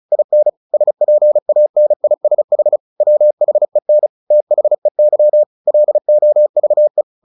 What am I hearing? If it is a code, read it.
Morse code, heard as a keyed tone, about 34 wpm, 600 Hz. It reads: IN SPANISH WHEN THEY LOVE
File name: the old man and the sea / the old man and the sea 1 35wpm